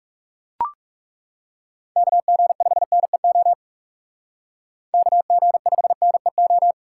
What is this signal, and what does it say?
Morse code, heard as a keyed tone, about 36 wpm, 700 Hz. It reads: KG5NEO KG5NEO
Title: Speed Racing - USA 2x3 Call Signs 30wpm